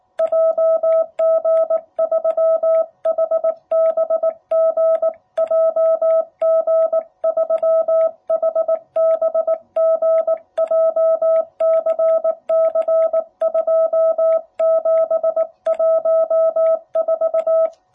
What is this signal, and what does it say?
Morse code, heard as a keyed tone, about 19 wpm, 660 Hz. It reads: JG3HBG JG3HBG JCC2714
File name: jh8cbh / 2021cw